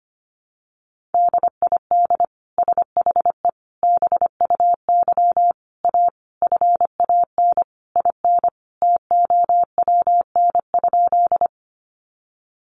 Morse code, heard as a keyed tone, about 25 wpm, 700 Hz. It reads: DID SHE BUY A FAN IN TOWN?
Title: Sentences from Top 1300 Words 25wpm